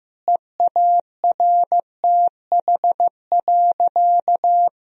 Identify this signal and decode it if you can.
Morse code, heard as a keyed tone, about 15 wpm, 700 Hz. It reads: EARTH.
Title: Sentences from Top 1000 Words - Review Part 3 - 15wpm